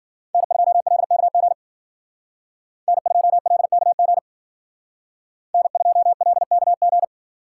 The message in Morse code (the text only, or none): N2LKG N2LKG N2LKG